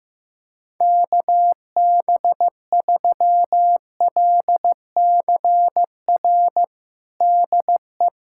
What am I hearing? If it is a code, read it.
Morse code, heard as a keyed tone, about 15 wpm, 700 Hz. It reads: KB3LCR DE
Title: USA Callsign DE Callsign - 6 Characters - 15wpm